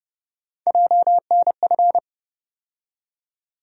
Morse code, heard as a keyed tone, about 30 wpm, 700 Hz. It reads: JNF
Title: USA Call Sign Postfixes - 3 Letters 30wpm